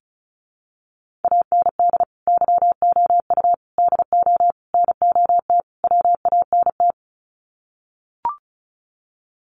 Morse code, heard as a keyed tone, about 34 wpm, 700 Hz. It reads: AND YOU DO NOT WANT
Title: Sets of 5 Words Encore2 35wpm